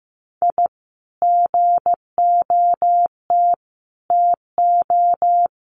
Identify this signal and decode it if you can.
Morse code, heard as a keyed tone, about 15 wpm, 700 Hz. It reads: I GOT TO
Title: Sets of 5 Words Encore2 15wpm